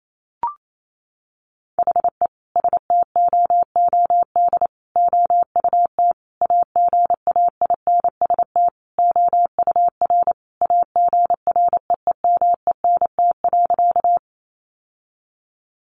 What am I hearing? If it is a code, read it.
Morse code, heard as a keyed tone, about 28 wpm, 700 Hz. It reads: HE STOOD OUT AGAINST OUR AGREEMENT.
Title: Sentences from Top 1500 Words 28wpm